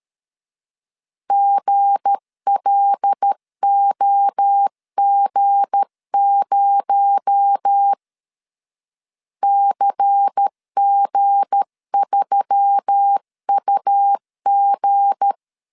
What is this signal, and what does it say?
Morse code, heard as a keyed tone, about 13 wpm, 800 Hz. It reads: GLOG0 CG3UG
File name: LEZIONI / 7 GF9Z / 7 G 12wpm 2055